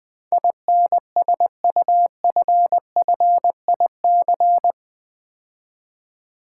Morse code, heard as a keyed tone, about 20 wpm, 700 Hz. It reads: INSUFFIC
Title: Build A Word - 4 Letters At A Time - 12-Letter Words - 20wpm